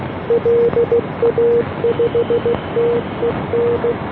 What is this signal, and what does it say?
Morse code, heard as a keyed tone, about 16 wpm, 470 Hz, with noise. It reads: LA5TEN